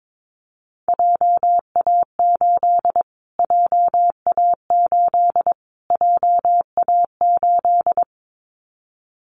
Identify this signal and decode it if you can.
Morse code, heard as a keyed tone, about 22 wpm, 700 Hz. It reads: JA8 JA8 JA8
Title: Callsign Prefixes for Top 25 DXCC - Repeated 3x - 22wpm